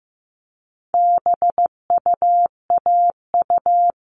Morse code, heard as a keyed tone, about 15 wpm, 700 Hz. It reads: BUAU
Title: All Cut Numbers - 4-Digit 15wpm